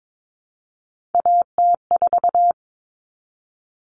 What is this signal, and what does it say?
Morse code, heard as a keyed tone, about 22 wpm, 700 Hz. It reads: AT4
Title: Common Cut Numbers - 3-Digit 22wpm